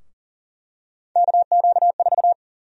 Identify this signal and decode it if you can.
Morse code, heard as a keyed tone, about 40 wpm, 700 Hz. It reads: KQ4
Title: USA Call Sign Prefixes - 40wpm